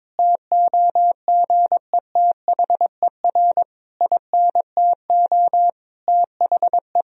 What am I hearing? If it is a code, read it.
Morse code, heard as a keyed tone, about 22 wpm, 700 Hz. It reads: TOGETHER INTO THE
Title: Sentences from Top 1000 Words - Review Part 3 - 22wpm